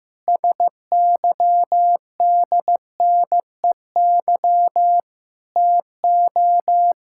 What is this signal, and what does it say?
Morse code, heard as a keyed tone, about 15 wpm, 700 Hz. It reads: SYDNEY TO
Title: Pangram Challenge v2 - 15wpm